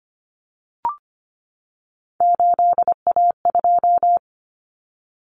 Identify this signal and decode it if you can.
Morse code, heard as a keyed tone, about 25 wpm, 700 Hz. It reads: 8A2
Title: Common Cut Numbers - 3-Digit 25wpm